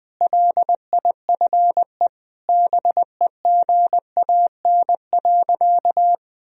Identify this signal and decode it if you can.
Morse code, heard as a keyed tone, about 20 wpm, 700 Hz. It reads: LIFE BEGAN.
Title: Sentences from Top 1000 Words - Review Part 1 - 20wpm